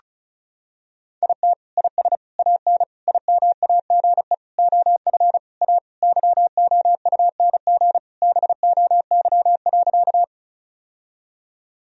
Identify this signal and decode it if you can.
Morse code, heard as a keyed tone, about 36 wpm, 700 Hz. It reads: IT IS AN IMAGE OF A YOUNG BOY.
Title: Sentences from Top 1000 Words - Review Part 4 - 35wpm